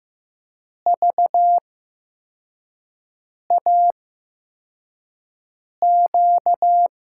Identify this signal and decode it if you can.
Morse code, heard as a keyed tone, about 15 wpm, spaced as for 4 wpm, 700 Hz. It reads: VAQ
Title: USA Call Signs - 8x Character Spacing - 15wpm